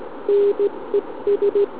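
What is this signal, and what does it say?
Morse code, heard as a keyed tone, about 17 wpm, 400 Hz. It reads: NES
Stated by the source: SM7ALC heard via tropo on 144 MHz with an FT817 and a quarter-wave antenna on a central-heating radiator.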